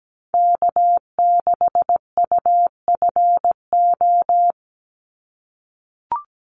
Morse code, heard as a keyed tone, about 17 wpm, 700 Hz. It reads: K6UFO
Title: Call Signs - 17wpm